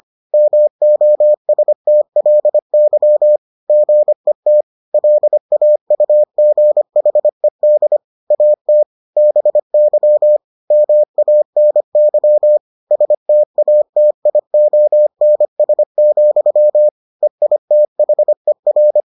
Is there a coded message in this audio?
Yes: Morse code from 0.3 to 19.0 s, heard as a keyed tone, about 25 wpm, 600 Hz. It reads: MOSTLY GET LAUGHED AT BY MANY STATIONS, EITHER